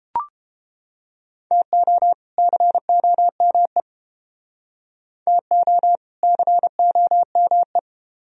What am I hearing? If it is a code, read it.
Morse code, heard as a keyed tone, about 31 wpm, 700 Hz. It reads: TO COME TO COME